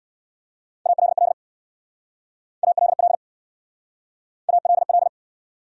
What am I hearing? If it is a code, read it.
Morse code, heard as a keyed tone, about 67 wpm, 700 Hz. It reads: ALL ALL ALL